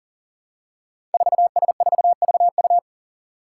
Morse code, heard as a keyed tone, about 39 wpm, 700 Hz. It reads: 4S4VU
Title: Five Challenging Letter-Number Combinations 40wpm